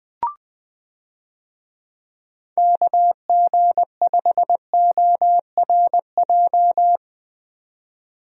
Morse code, heard as a keyed tone, about 21 wpm, 700 Hz. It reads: KG5ORJ
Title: USA Call Signs - 6 Characters - 20wpm